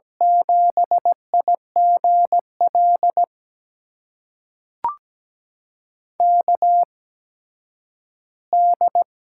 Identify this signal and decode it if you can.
Morse code, heard as a keyed tone, about 17 wpm, spaced as for 10 wpm, 700 Hz. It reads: 7IGL KD
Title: USA Call Signs - 6 Characters - 8x Character Spacing - 17wpm